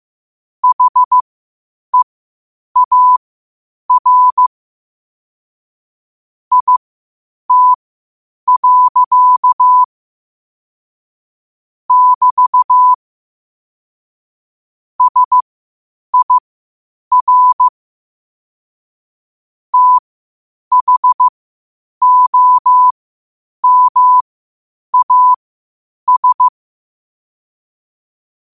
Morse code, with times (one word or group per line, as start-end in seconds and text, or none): 0.6-4.5 HEAR
6.5-9.8 IT.
11.9-12.9 <BT>
15.0-17.7 SIR
19.7-26.5 THOMAS